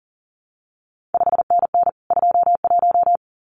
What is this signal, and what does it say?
Morse code, heard as a keyed tone, about 39 wpm, 700 Hz. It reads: 5NN 21
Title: CQ WW DX Contest Exchange Encore - 40wpm